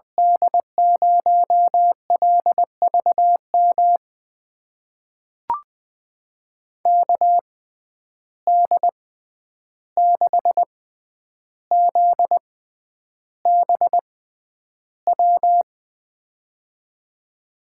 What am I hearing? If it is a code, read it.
Morse code, heard as a keyed tone, about 21 wpm, spaced as for 10 wpm, 700 Hz. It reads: D0LVM KD6ZBW
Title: USA Call Signs - 6 Characters - 6x Character Spacing - 20wpm